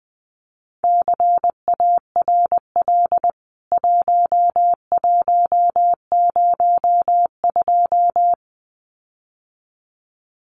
Morse code, heard as a keyed tone, about 20 wpm, 700 Hz. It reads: CARL 1102
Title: CWT Exchange 20wpm